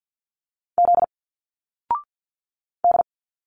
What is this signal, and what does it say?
Morse code, heard as a keyed tone, about 51 wpm, 700 Hz. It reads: Z D
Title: Single Letters 50wpm